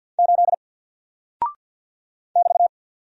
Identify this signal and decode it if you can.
Morse code, heard as a keyed tone, about 48 wpm, 700 Hz. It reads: 8 <BT>
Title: Single Character (Letter:Number:Punctuation:Prosign) Alt 50wpm